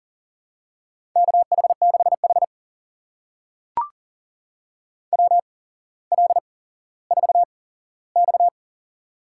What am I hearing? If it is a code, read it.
Morse code, heard as a keyed tone, about 38 wpm, spaced as for 17 wpm, 700 Hz. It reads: KH6H WL4X
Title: USA Call Signs - 8x Character Spacing - 40wpm